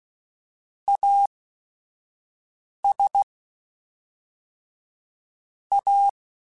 Morse code, heard as a keyed tone, about 16 wpm, spaced as for 5 wpm, 800 Hz. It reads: AS A
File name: Technical Subjects / Technical Subjects-mp3s / XC @5